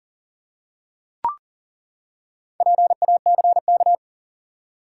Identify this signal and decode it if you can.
Morse code, heard as a keyed tone, about 40 wpm, 700 Hz. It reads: PACK
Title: Four Letter Words - 40wpm